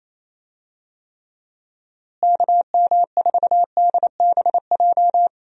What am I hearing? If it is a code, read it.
Morse code, heard as a keyed tone, about 28 wpm, 700 Hz. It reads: KM4DBJ
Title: USA Call Signs - 6 Characters - Repeated 2x - 28wpm